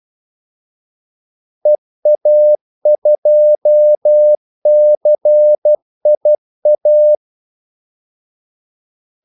Morse code, heard as a keyed tone, about 12 wpm, 600 Hz. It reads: EA2CIA
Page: 2 ekstra ordmellemrum mellem kaldesignalerne